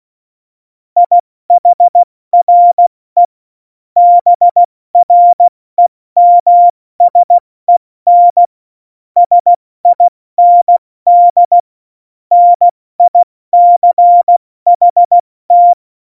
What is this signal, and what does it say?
Morse code, heard as a keyed tone, about 16 wpm, 700 Hz. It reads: IHRE BREMSEN SIND NICHT
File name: Morsekurs / Witze / Witz 07 Juli 2020, 70 F80